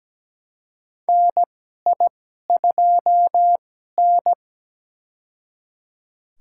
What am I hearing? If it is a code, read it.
Morse code, heard as a keyed tone, about 17 wpm, spaced as for 12 wpm, 700 Hz. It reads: NI2N
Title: USA Call Signs - 2x Character Spacing - 17wpm